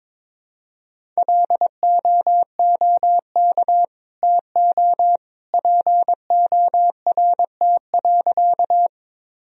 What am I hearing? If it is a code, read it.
Morse code, heard as a keyed tone, about 22 wpm, 700 Hz. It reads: LOOK TO PORT.
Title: Sentences from Top 1000 Words - Review Part 1 - 22wpm